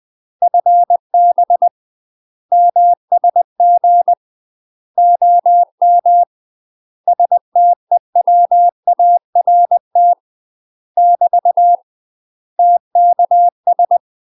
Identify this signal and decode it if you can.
Morse code, heard as a keyed tone, about 20 wpm, 700 Hz. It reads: FB MSG OM STEWART <BT> TKS
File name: UK MASTER QSOs TRAINING/20WPM